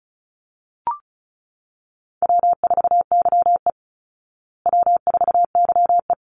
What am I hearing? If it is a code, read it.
Morse code, heard as a keyed tone, about 34 wpm, 700 Hz. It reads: W4YE W4YE
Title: Call Signs Repeated 3x - 35wpm